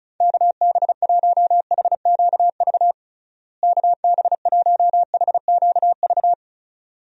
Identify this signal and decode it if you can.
Morse code, heard as a keyed tone, about 36 wpm, 700 Hz. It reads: KB1HQV KB1HQV
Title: USA Call Signs - 6 Characters - Repeated 2x - 35wpm